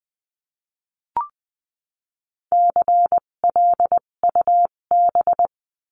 Morse code, heard as a keyed tone, about 20 wpm, spaced as for 17 wpm, 700 Hz. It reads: CLUB
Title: 257 Lesson 42b - Words 14-20wpm